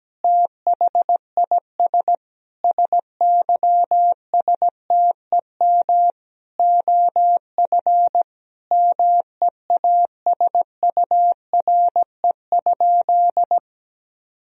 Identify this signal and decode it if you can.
Morse code, heard as a keyed tone, about 17 wpm, 700 Hz. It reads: THIS SYSTEM OF MEASURE?
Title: Sentences from Top 1000 Words - Review Part 3 - 17wpm